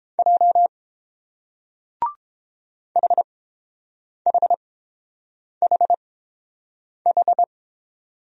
Morse code, heard as a keyed tone, about 30 wpm, spaced as for 7 wpm, 700 Hz. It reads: J HHHH